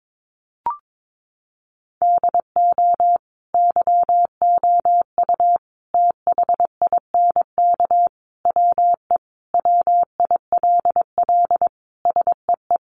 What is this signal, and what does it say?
Morse code, heard as a keyed tone, about 22 wpm, 700 Hz. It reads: DO YOU THINK WE WILL SEE
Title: Sentences from Top 1200 Words 22wpm